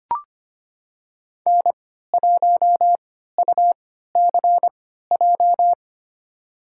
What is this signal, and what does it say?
Morse code, heard as a keyed tone, about 25 wpm, spaced as for 14 wpm, 700 Hz. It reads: N1UCJ
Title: USA Call Signs - 5 Characters - 3x Character Spacing - 25wpm